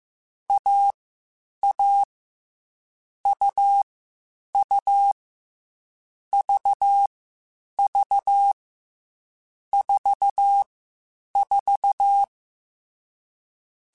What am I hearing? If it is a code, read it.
Morse code, heard as a keyed tone, about 15 wpm, 800 Hz. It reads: A A U U V V 4 4
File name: Short Runs / Short Runs-mp3s / AUV AUV4 @13